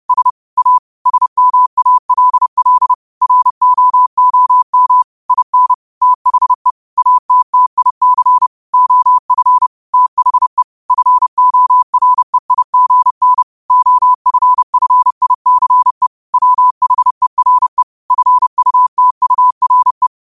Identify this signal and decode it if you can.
Morse code, heard as a keyed tone, about 30 wpm, 1000 Hz. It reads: S A SMALL ROOM IN THE ATTIC OF THE FOREIGN OFFICE WHERE FUTURE
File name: qotdfiles / 30wpm